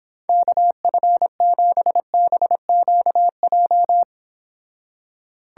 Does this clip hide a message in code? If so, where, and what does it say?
Yes: Morse code from 0.3 to 4.0 s, heard as a keyed tone, about 26 wpm, 700 Hz. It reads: KF7BQJ